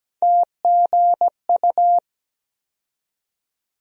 Morse code, heard as a keyed tone, about 17 wpm, 700 Hz. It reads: TGU
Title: USA Call Sign Postfixes - 3 Letters 17wpm